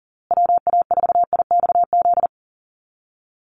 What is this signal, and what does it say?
Morse code, heard as a keyed tone, about 40 wpm, 700 Hz. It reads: WA4IXZ
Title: USA Call Signs Repeated 3x 40wpm